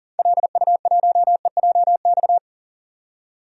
Morse code, heard as a keyed tone, about 40 wpm, 700 Hz. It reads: LU1EJX
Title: DXCC-100 Call Signs - 6 Characters - 40wpm